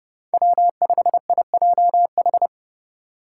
Morse code, heard as a keyed tone, about 32 wpm, 700 Hz. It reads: W5IJH
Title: K1USN SST QSO Practice - 30wpm